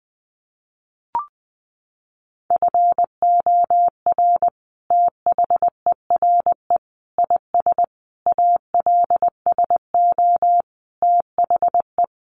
Morse code, heard as a keyed tone, about 20 wpm, 700 Hz. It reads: FOR THERE IS ALSO THE
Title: Sentences from Top 1400 Words 20wpm